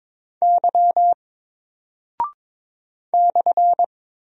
Morse code, heard as a keyed tone, about 22 wpm, 700 Hz. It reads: Y /
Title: Single Character (Letter:Number:Punctuation:Prosign) 22wpm